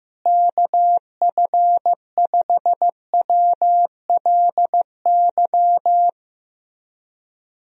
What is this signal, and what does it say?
Morse code, heard as a keyed tone, about 15 wpm, 700 Hz. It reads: KF5WLY
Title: USA Call Signs - 6 Characters - Repeated 3x - 15wpm